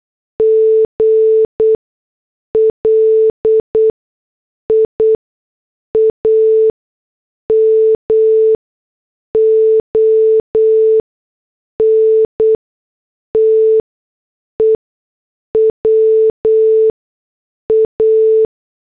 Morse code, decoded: GLIA MONTE WA